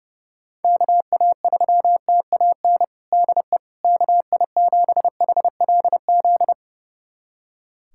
Morse code, heard as a keyed tone, about 30 wpm, 700 Hz. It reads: KA3TAN DE KI7HLZ
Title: USA Callsign DE Callsign - 6 Characters - 30wpm